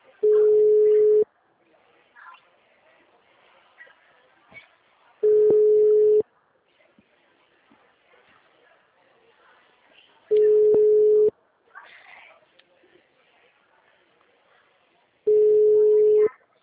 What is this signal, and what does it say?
Ringback tone (Europe, 425 Hz), with speech.